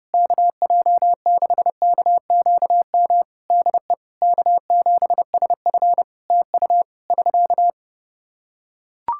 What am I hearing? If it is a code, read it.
Morse code, heard as a keyed tone, about 30 wpm, 700 Hz. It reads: KJ6KQM DE K7SF TU <SK>
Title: Practice QSO - Exchange 4 - Ending - 30wpm